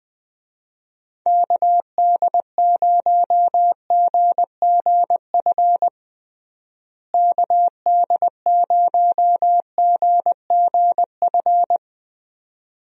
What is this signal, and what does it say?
Morse code, heard as a keyed tone, about 20 wpm, 700 Hz. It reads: KD0GGF KD0GGF
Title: USA Call Signs - 6 Characters - Repeated 2x - 20wpm